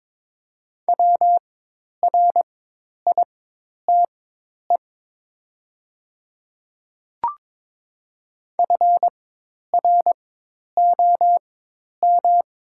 Morse code, heard as a keyed tone, about 22 wpm, spaced as for 10 wpm, 700 Hz. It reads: WRITE FROM
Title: 4x Character Spacing - Top 100 Words - 22wpm